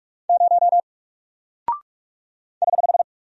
Morse code, heard as a keyed tone, about 44 wpm, 700 Hz. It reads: 0 <HH>
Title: Single Character (Letter:Number:Punctuation:Prosign) Alt 45wpm